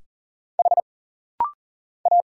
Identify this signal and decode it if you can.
Morse code, heard as a keyed tone, about 40 wpm, 700 Hz. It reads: H A
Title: Single Letter-Number Alt - Rapid-Fire 40wpm